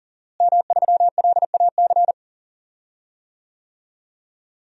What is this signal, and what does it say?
Morse code, heard as a keyed tone, about 40 wpm, 700 Hz. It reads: M3LAC
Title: DXCC-100 Call Signs - 5 Characters - Repeated 3x - 40wpm